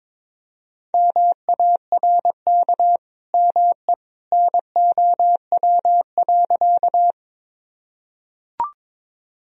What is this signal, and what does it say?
Morse code, heard as a keyed tone, about 22 wpm, 700 Hz. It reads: MARK ME NOW.
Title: Sentences from Top 1000 Words - Review Part 1 - 22wpm